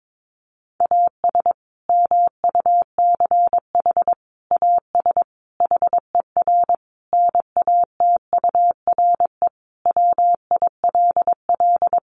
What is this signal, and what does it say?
Morse code, heard as a keyed tone, about 22 wpm, 700 Hz. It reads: AS MUCH AS HER NATURE WILL
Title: Sentences from Top 700 Words 22wpm